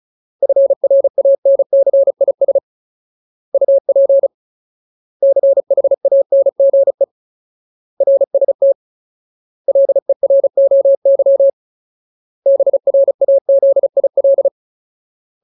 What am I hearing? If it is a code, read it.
Morse code, heard as a keyed tone, about 35 wpm, 550 Hz. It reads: FRANCIS UP CHANGE RST LEROY BRAZIL
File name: Ultimate Word List / 35WPM Ultimate